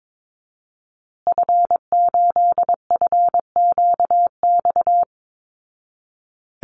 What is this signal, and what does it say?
Morse code, heard as a keyed tone, about 22 wpm, 700 Hz. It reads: F8FQX
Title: Call Signs Repeated 3x - 22wpm